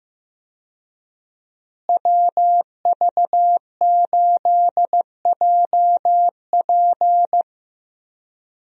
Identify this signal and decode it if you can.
Morse code, heard as a keyed tone, about 15 wpm, 700 Hz. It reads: WV8JP
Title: USA Call Signs - 5 Characters - Repeated 3x - 15wpm